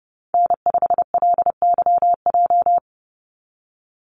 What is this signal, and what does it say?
Morse code, heard as a keyed tone, about 30 wpm, 700 Hz. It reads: N5LYJ
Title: Call Signs Repeated 3x - 30wpm